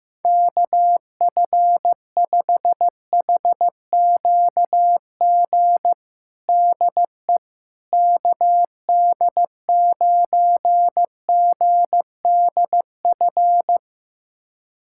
Morse code, heard as a keyed tone, about 15 wpm, 700 Hz. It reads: KF5HQG DE KD9GDF
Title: USA Callsign DE Callsign 15wpm